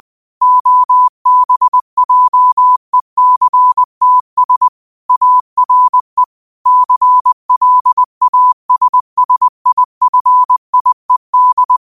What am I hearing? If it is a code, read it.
Morse code, heard as a keyed tone, about 20 wpm, 1000 Hz. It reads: OBJECTS ARE CLASSIFIED
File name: qotdfiles / 20wpm